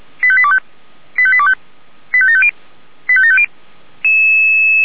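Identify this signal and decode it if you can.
Five-tone selective call (ZVEI), heard as quick runs of five tones, tones of about 70 ms. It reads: 85825 85825 85750 85750
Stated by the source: ZVEI five-tone selective call received on an SDR.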